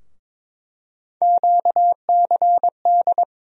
Morse code, heard as a keyed tone, about 22 wpm, 700 Hz. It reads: QCD
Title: USA Call Sign Postfixes - 3 Letters 22wpm